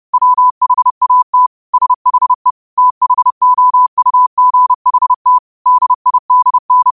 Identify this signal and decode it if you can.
Morse code, heard as a keyed tone, about 30 wpm, 1000 Hz. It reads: WHAT SHE THOUGHT DIDN
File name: qotdfiles / 30wpm